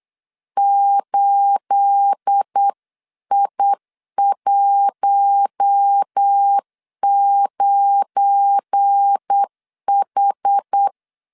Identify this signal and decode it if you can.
Morse code, heard as a keyed tone, about 8 wpm, 800 Hz. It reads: 8I19H
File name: CW / DET / 1 9 1500C S C 8wpm 5653